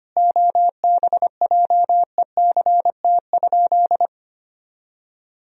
Morse code, heard as a keyed tone, about 25 wpm, 700 Hz. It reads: OBJECT?
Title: Sentences from Top 1000 Words - Review Part 3 - 25wpm